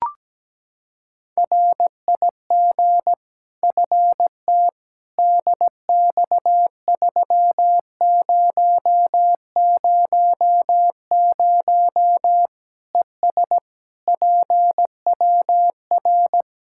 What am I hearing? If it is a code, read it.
Morse code, heard as a keyed tone, about 17 wpm, 700 Hz. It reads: RIG FT DX3000 ES PWR
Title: Practice QSO - Exchange 2 - Info - 17wpm